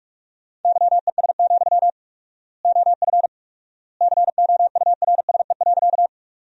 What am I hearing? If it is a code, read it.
Morse code, heard as a keyed tone, about 43 wpm, 700 Hz. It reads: YES, OF COURSE.
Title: 4x Word-Spacing - Sentences from Top 500 Words - 45wpm